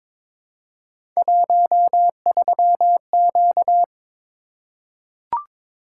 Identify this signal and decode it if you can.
Morse code, heard as a keyed tone, about 22 wpm, 700 Hz. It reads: 13Q
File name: practice / 22wpm / Three Letter-Number Combinations - 22wpm